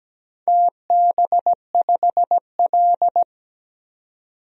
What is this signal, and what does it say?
Morse code, heard as a keyed tone, about 17 wpm, 700 Hz. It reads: TB5L
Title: DXCC-100 Call Signs - 4 Characters - 17wpm